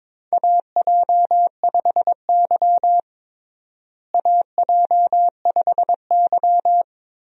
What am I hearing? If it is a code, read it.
Morse code, heard as a keyed tone, about 22 wpm, 700 Hz. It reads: AJ5Y AJ5Y
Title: USA Call Signs - 4 Characters - Repeated 3x - 22wpm